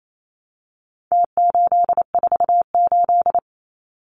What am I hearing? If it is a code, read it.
Morse code, heard as a keyed tone, about 28 wpm, 700 Hz. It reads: T848
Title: Numbers - Contest Style - 4-Digit 28wpm